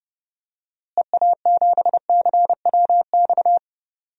Morse code, heard as a keyed tone, about 30 wpm, 700 Hz. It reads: EA7CWX
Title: DXCC-100 Call Signs - 6 Characters - Repeated 2x - 30wpm